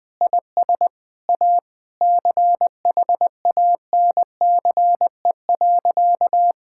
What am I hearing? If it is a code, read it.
Morse code, heard as a keyed tone, about 20 wpm, 700 Hz. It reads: IS A CHANCE.
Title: Sentences with 4-Word Phrases - 20wpm